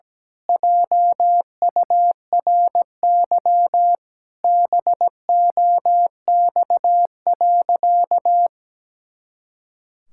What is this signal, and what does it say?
Morse code, heard as a keyed tone, about 17 wpm, 700 Hz. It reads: JURY BOX.
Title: Pangram Challenge v2 - 17wpm